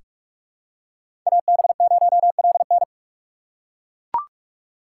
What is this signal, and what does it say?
Morse code, heard as a keyed tone, about 41 wpm, 700 Hz. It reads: AB0LN
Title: USA Call Signs - 5 Characters - 6x Character Spacing - 45wpm